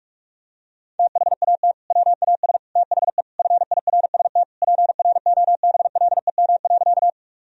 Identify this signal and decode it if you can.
Morse code, heard as a keyed tone, about 44 wpm, 700 Hz. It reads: THAT WAS THE FIRST PROBLEM.
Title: Sentences from Top 1000 Words - Review Part 2 - 45wpm